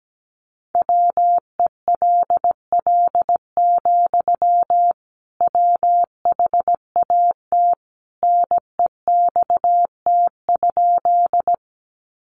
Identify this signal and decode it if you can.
Morse code, heard as a keyed tone, about 17 wpm, 700 Hz. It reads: WELL, WHAT NEXT?
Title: Sentences from Top 300 Words 17wpm